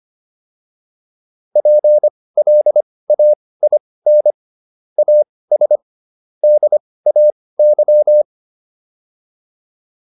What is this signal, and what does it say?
Morse code, heard as a keyed tone, about 25 wpm, spaced as for 18 wpm, 600 Hz. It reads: PLAIN AS DAY